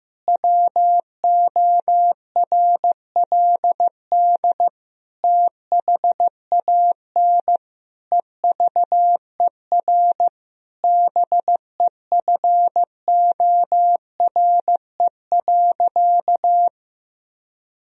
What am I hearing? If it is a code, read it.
Morse code, heard as a keyed tone, about 15 wpm, 700 Hz. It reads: WORLD THAN EVER BEFORE.
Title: Sentences from Top 1000 Words - Review Part 3 - 15wpm